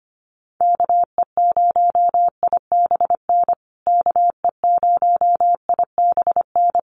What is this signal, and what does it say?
Morse code, heard as a keyed tone, about 25 wpm, 700 Hz. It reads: KE0IBN KE0IBN
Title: USA Call Signs Repeated 3x 25wpm